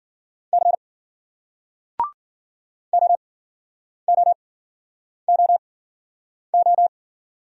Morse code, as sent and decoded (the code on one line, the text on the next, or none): -.-. / --- --- --- ---
C OOOO